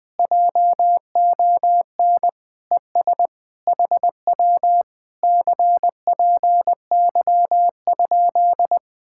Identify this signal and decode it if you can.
Morse code, heard as a keyed tone, about 20 wpm, 700 Hz. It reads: JON ES HW CPY?
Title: Speed-Racing Common QSO Phrases Encore - 17wpm